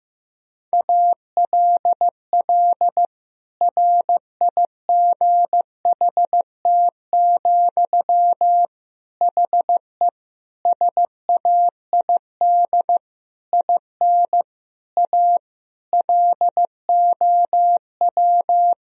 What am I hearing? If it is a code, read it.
Morse code, heard as a keyed tone, about 15 wpm, 700 Hz. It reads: ALL RIGHT, HE SAID IN A LOW
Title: Sentences from Top 1000 Words - Review Part 1 - 15wpm